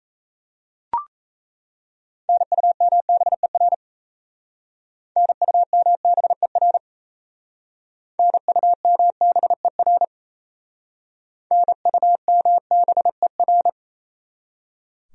Morse code, heard as a keyed tone, about 36 wpm, 700 Hz. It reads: NUMBER NUMBER NUMBER NUMBER